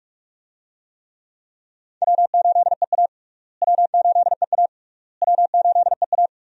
Morse code, heard as a keyed tone, about 43 wpm, 700 Hz. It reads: W8EA W8EA W8EA
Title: USA Call Signs - 4 Characters - Repeated 3x - 45wpm